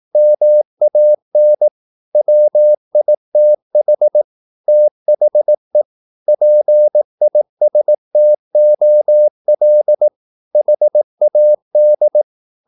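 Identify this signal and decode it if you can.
Morse code, heard as a keyed tone, about 18 wpm, 600 Hz. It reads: MAN WITH THE PISTOL HAD
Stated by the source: Morse practice at 18 wpm, English text, from CW med Gnister.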